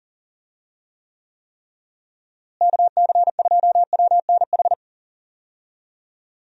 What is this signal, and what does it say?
Morse code, heard as a keyed tone, about 40 wpm, 700 Hz. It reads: KC2WNH
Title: USA Call Signs - 6 Characters - Repeated 3x - 40wpm